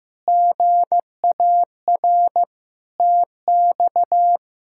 Morse code, heard as a keyed tone, about 15 wpm, 700 Hz. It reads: GAR TX
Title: K1USN SST QSO Practice - 15wpm